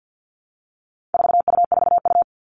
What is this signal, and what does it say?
Morse code, heard as a keyed tone, about 50 wpm, 700 Hz. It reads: 4U4U
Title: All Cut Numbers - 4-Digit 50wpm